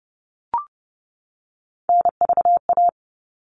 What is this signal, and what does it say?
Morse code, heard as a keyed tone, about 30 wpm, 700 Hz. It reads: NVA
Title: All Cut Numbers - 3-Digit 30wpm